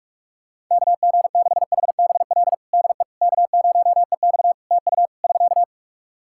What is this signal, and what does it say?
Morse code, heard as a keyed tone, about 48 wpm, 700 Hz. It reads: KG6HBL DE K0EX TU <SK>
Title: K1USN SST QSO Practice 45wpm